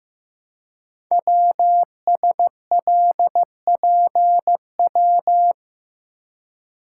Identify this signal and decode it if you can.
Morse code, heard as a keyed tone, about 15 wpm, 700 Hz. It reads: WSLPW
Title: Code Groups - 5 Letters - 15wpm